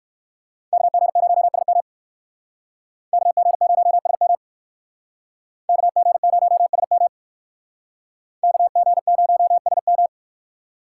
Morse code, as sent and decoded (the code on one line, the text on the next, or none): -.- --. ----- ... -- / -.- --. ----- ... -- / -.- --. ----- ... -- / -.- --. ----- ... --
KG0SM KG0SM KG0SM KG0SM